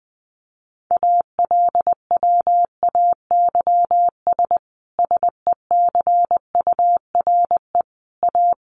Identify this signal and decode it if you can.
Morse code, heard as a keyed tone, about 20 wpm, 700 Hz. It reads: ALWAYS SECURE A
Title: Sentences from Top 1200 Words 20wpm